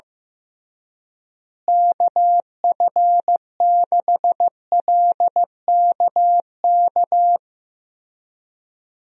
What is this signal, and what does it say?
Morse code, heard as a keyed tone, about 15 wpm, 700 Hz. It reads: KF6LKK
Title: USA Call Signs - 6 Characters - Repeated 2x - 15wpm